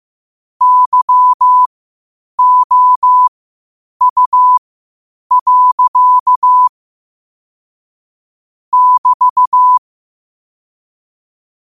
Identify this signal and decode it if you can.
Morse code, heard as a keyed tone, about 15 wpm, spaced as for 8 wpm, 1000 Hz. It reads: YOU. <BT>
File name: qotdfiles / 8wpm